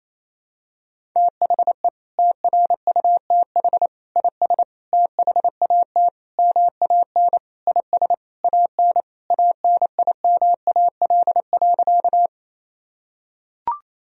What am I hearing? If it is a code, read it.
Morse code, heard as a keyed tone, about 28 wpm, 700 Hz. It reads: THE TRUTH IS THAT MAN IS AN ANIMAL.
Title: Sentences from Top 1000 Words - Review Part 1 - 28wpm